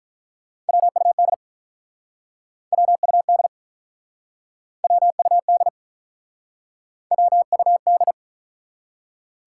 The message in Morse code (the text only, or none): WUD WUD WUD WUD